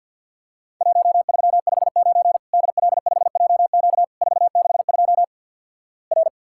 Morse code, heard as a keyed tone, about 50 wpm, 700 Hz. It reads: 1359 DL5JQ 462 R
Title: Worked All Europe DX Contest - QTC Exchange - 50wpm